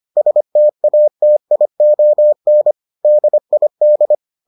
Morse code, heard as a keyed tone, about 25 wpm, 600 Hz. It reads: STATION DID